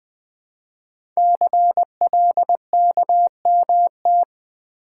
Morse code, heard as a keyed tone, about 20 wpm, 700 Hz. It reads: CLKMT
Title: Code Groups - 5 Letters - 20wpm